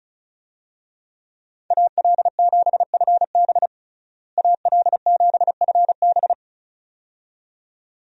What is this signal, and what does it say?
Morse code, heard as a keyed tone, about 36 wpm, 700 Hz. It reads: AL7FB AL7FB
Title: USA Call Signs - 5 Characters - Repeated 2x - 35wpm